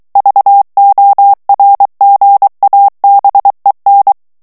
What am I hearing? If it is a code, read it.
Morse code, heard as a keyed tone, about 23 wpm, 800 Hz. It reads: VORGABEN
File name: ALLEMAND texte clair 1 vitesse 22 mots minute